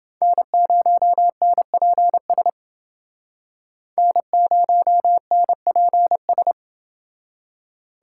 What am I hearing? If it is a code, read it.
Morse code, heard as a keyed tone, about 29 wpm, 700 Hz. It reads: N0NPS N0NPS